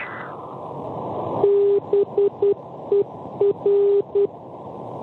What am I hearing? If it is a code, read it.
Morse code, heard as a keyed tone, about 10 wpm, 400 Hz. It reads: BER
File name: BER NDB Berna Svizzera 335 kHz